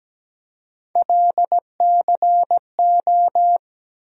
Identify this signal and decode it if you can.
Morse code, heard as a keyed tone, about 17 wpm, 700 Hz. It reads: LCO
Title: USA Call Sign Postfixes - 3 Letters 17wpm